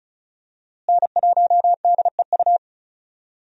Morse code, heard as a keyed tone, about 34 wpm, 700 Hz. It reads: N1DEU
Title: USA Call Signs - 5 Characters - Repeated 2x - 35wpm